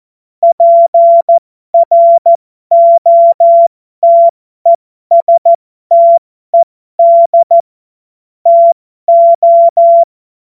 Morse code, heard as a keyed tone, about 14 wpm, spaced as for 12 wpm, 680 Hz. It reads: PROTESTED TO
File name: Morsekurs / Witze